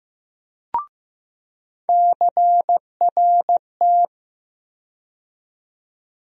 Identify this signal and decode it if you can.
Morse code, heard as a keyed tone, about 15 wpm, 700 Hz. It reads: CRT
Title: USA Call Sign Postfixes - 3 Letters 15wpm